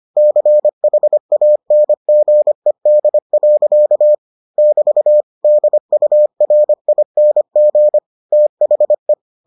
Morse code, heard as a keyed tone, about 25 wpm, 600 Hz. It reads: CHANGED. <BT> DURING THE